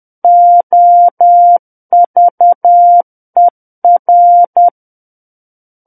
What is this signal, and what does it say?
Morse code, heard as a keyed tone, about 10 wpm, 700 Hz. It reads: OVER
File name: Book One / 01-Chapter 1 - 10 WPM